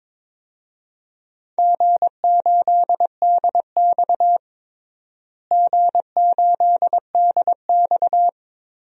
Morse code, heard as a keyed tone, about 22 wpm, 700 Hz. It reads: G8DX G8DX
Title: DXCC-100 Call Signs - 4 Characters - Repeated 2x - 22wpm